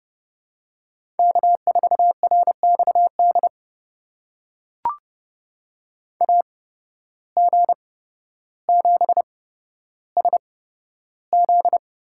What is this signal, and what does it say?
Morse code, heard as a keyed tone, about 31 wpm, spaced as for 13 wpm, 700 Hz. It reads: K4RXD AG7SZ
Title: USA Call Signs - 5 Characters - 8x Character Spacing - 30wpm